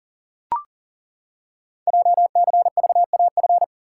Morse code, heard as a keyed tone, about 39 wpm, 700 Hz. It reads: JCVAF
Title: Code Groups - 5 Letters - 40wpm